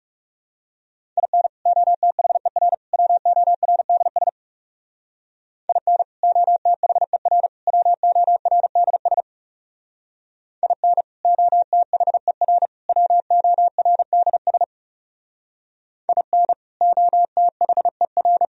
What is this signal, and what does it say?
Morse code, heard as a keyed tone, about 36 wpm, 700 Hz. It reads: IN OTHER WORDS IN OTHER WORDS IN OTHER WORDS IN OTHER
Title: Speed-Racing - Sets of 3 Words 30wpm